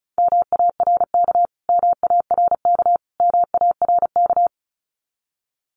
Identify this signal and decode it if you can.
Morse code, heard as a keyed tone, about 34 wpm, 700 Hz. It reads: MARK MARK MARK
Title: Top 500 Words Repeated 35wpm